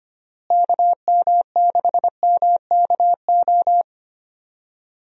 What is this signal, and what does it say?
Morse code, heard as a keyed tone, about 25 wpm, 700 Hz. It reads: KM6MKO
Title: USA Call Signs - 6 Characters - Repeated 3x - 25wpm